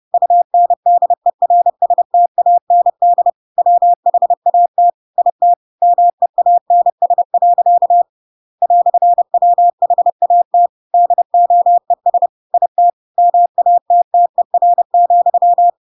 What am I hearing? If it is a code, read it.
Morse code, heard as a keyed tone, about 30 wpm, 700 Hz. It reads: UNDERSTAND WHAT IT MEANS. "WHAT DOES IT MATTER,
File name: Pinocchio / Pinocchio ch8 30 30 ik1hge